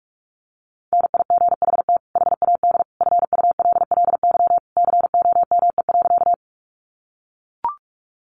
Morse code, heard as a keyed tone, about 44 wpm, 700 Hz. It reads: NIGHT HAD FULLY COME.
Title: Sentences from Top 1300 Words 45wpm